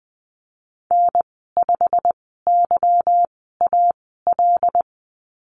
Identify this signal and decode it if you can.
Morse code, heard as a keyed tone, about 20 wpm, spaced as for 14 wpm, 700 Hz. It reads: N5YAL
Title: 147 Lesson 24g - Letter y - Call Signs 20wpm